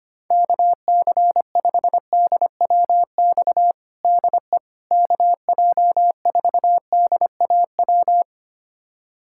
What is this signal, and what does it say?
Morse code, heard as a keyed tone, about 25 wpm, 700 Hz. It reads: KC5DWX DE KJ4DAW
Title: USA Callsign DE Callsign - 6 Characters - 25wpm